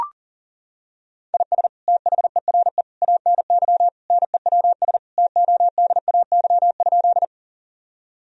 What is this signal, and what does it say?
Morse code, heard as a keyed tone, about 37 wpm, 700 Hz. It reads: IS THERE ANY NEWS TODAY?
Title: Sentences from Top 1000 Words - Review Part 4 - 40wpm